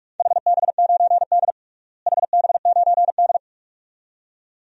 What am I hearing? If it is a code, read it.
Morse code, heard as a keyed tone, about 46 wpm, 700 Hz. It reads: HB9D HB9D
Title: DXCC-100 Call Signs - 4 Characters - Repeated 2x - 45wpm